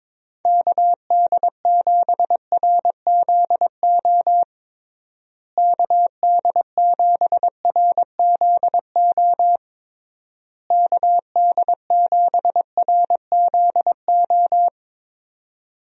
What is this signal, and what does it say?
Morse code, heard as a keyed tone, about 22 wpm, 700 Hz. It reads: KD7RZO KD7RZO KD7RZO
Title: USA Call Signs - 6 Characters - Repeated 3x - 22wpm